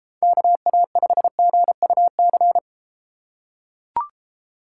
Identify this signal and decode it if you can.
Morse code, heard as a keyed tone, about 33 wpm, 700 Hz. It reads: KA5GUC